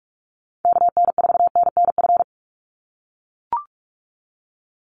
Morse code, heard as a keyed tone, about 44 wpm, 700 Hz. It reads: KN4NNF
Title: USA Call Signs 45wpm